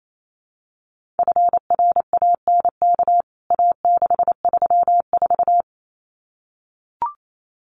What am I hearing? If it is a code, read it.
Morse code, heard as a keyed tone, about 28 wpm, 700 Hz. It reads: FRANK A634
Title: CWT Exchange 28wpm